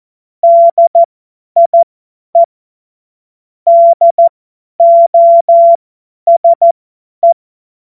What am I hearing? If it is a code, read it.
Morse code, heard as a keyed tone, about 14 wpm, spaced as for 10 wpm, 680 Hz. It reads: DIE DOSE
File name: Morsekurs / Witze / tz 08. November 2022 50 F70